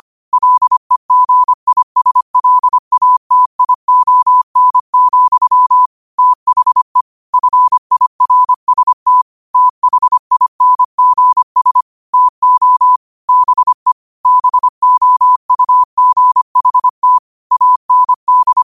Morse code, heard as a keyed tone, about 25 wpm, 1000 Hz. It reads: LEGISLATION, THE FIRST THINGS TO BE BOUGHT AND
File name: qotdfiles / 25wpm